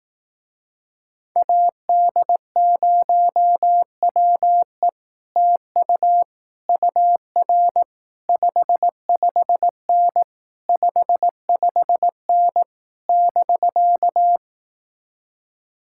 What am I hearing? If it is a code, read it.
Morse code, heard as a keyed tone, about 18 wpm, 700 Hz. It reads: AD0WE TU UR 55N 55N <BK>
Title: Speed-Racing Common QSO Phrases Encore - 15wpm